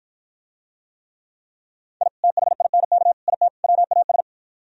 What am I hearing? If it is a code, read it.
Morse code, heard as a keyed tone, about 52 wpm, 700 Hz. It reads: I THINK IT WAS